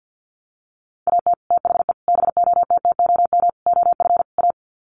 Morse code, heard as a keyed tone, about 50 wpm, 700 Hz. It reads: AT THE BOTTOM OF A
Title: Sets of 5 Words Encore2 50wpm